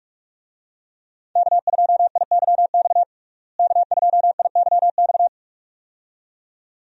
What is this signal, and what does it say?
Morse code, heard as a keyed tone, about 46 wpm, 700 Hz. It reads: K2IYX K2IYX
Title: USA Call Signs - 5 Characters - Repeated 2x - 45wpm